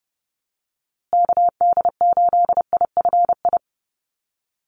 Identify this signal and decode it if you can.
Morse code, heard as a keyed tone, about 30 wpm, 700 Hz. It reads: KD8IFI
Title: USA Call Signs Repeated 2x 30wpm